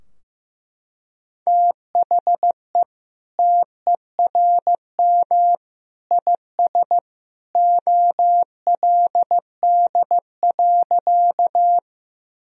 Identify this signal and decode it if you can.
Morse code, heard as a keyed tone, about 14 wpm, 700 Hz. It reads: THE TERM IS OLD.
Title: Sentences from Top 1000 Words - Review Part 4 - 15wpm